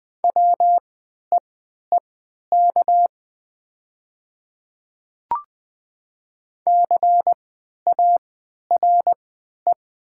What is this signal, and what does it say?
Morse code, heard as a keyed tone, about 20 wpm, spaced as for 11 wpm, 700 Hz. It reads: WEEK CARE
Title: 3x Character Spacing - Top 500 Words - 20wpm